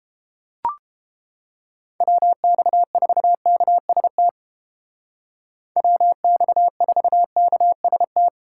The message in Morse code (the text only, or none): WX4KST WX4KST